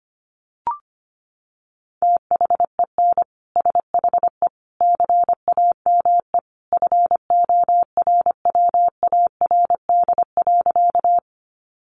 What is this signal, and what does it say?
Morse code, heard as a keyed tone, about 25 wpm, 700 Hz. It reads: THEN SHE CAME FORWARD.
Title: Sentences from Top 1300 Words 25wpm